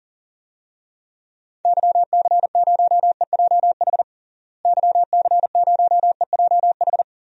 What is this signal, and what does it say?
Morse code, heard as a keyed tone, about 40 wpm, 700 Hz. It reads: YC0EJH YC0EJH
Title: DXCC-100 Call Signs - 6 Characters - Repeated 3x - 40wpm